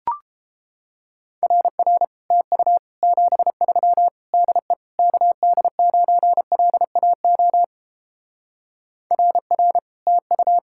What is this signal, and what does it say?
Morse code, heard as a keyed tone, about 29 wpm, 700 Hz. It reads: RR TU 73 DE KD9LAO RR TU